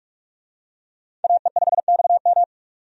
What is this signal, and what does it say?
Morse code, heard as a keyed tone, about 47 wpm, 700 Hz. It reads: AE5XM
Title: USA Call Signs - 5 Characters - 45wpm